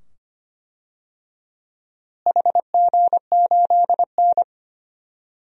Morse code, heard as a keyed tone, about 25 wpm, 700 Hz. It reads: HG8N
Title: DXCC-100 Call Signs - 4 Characters - Repeated 2x - 25wpm